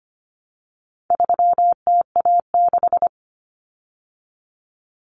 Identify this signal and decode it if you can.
Morse code, heard as a keyed tone, about 25 wpm, 700 Hz. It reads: 3TA6
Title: Common Cut Numbers - 4-Digit 25wpm